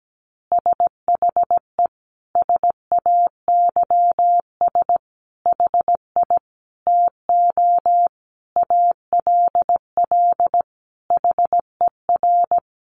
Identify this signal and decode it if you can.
Morse code, heard as a keyed tone, about 17 wpm, 700 Hz. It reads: SHE SAYS HI TO ALL HER
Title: Easy Phrases 17wpm